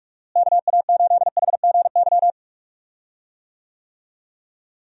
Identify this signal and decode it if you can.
Morse code, heard as a keyed tone, about 42 wpm, 700 Hz. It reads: KA8HGY
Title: USA Call Signs - 6 Characters - Repeated 2x - 45wpm